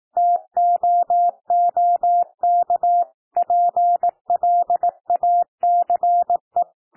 Morse code, heard as a keyed tone, about 18 wpm, 700 Hz. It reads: TOOK PLACE
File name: TEXT PASSAGE 18 wpm